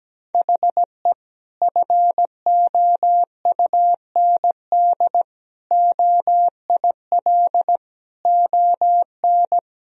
Morse code, heard as a keyed tone, about 17 wpm, 700 Hz. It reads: HE FOUND OIL ON
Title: Sentences from Top 1000 Words - Review Part 4 - 17wpm